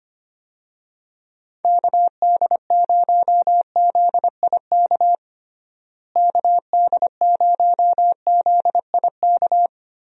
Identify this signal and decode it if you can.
Morse code, heard as a keyed tone, about 25 wpm, 700 Hz. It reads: KD0ZIK KD0ZIK
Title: USA Call Signs - 6 Characters - Repeated 3x - 25wpm